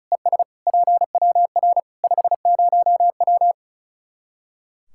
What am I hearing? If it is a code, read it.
Morse code, heard as a keyed tone, about 36 wpm, 700 Hz. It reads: ES PWR 50W
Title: Practice QSO - Exchange 2 - Info - 35wpm